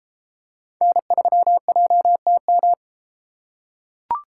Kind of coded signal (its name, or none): Morse code